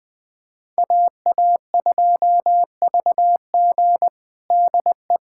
Morse code, heard as a keyed tone, about 20 wpm, 700 Hz. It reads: AA2VG DE
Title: USA Callsign DE Callsign - 5 Characters - 20wpm